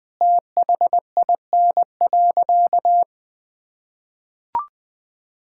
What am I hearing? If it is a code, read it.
Morse code, heard as a keyed tone, about 20 wpm, 700 Hz. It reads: THIN.
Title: Sentences from Top 1600 Words 20wpm